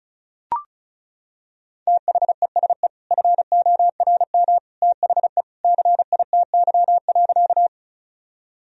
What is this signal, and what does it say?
Morse code, heard as a keyed tone, about 34 wpm, 700 Hz. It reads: THESE FORM THE CITY.
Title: Sentences from Top 1000 Words - Review Part 1 - 35wpm